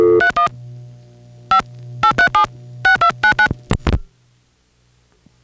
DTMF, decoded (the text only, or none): 31 5 837 6299